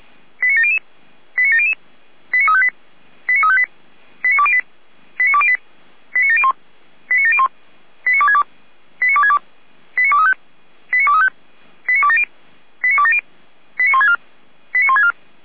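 Five-tone selective call (ZVEI), heard as quick runs of five tones, tones of about 70 ms. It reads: 89700 89700 89358 89358 89208 89208 89701 89701 89262 89262 89235 89235 89270 89270 89164 89164
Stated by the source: ZVEI five-tone call 89340 received on SDR.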